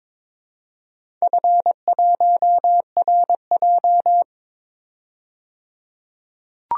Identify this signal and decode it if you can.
Morse code, heard as a keyed tone, about 22 wpm, 690 Hz. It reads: F1RJ
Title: DXCC-100 Call Signs - 4 Characters - 22wpm